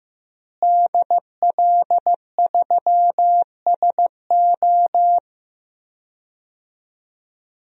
Morse code, heard as a keyed tone, about 16 wpm, 700 Hz. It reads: DL3SO
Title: DXCC-100 Call Signs - 5 Characters - Repeated 3x - 15wpm